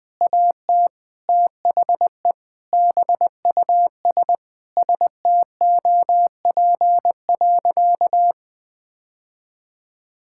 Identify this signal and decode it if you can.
Morse code, heard as a keyed tone, about 20 wpm, 700 Hz. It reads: AT THE BUS STOP.
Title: Sentences with 2-Word Phrases - 20wpm